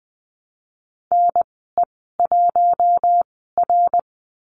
Morse code, heard as a keyed tone, about 20 wpm, spaced as for 15 wpm, 700 Hz. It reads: NE1R
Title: 055 Lesson 10g - Letter r - Call Signs 20wpm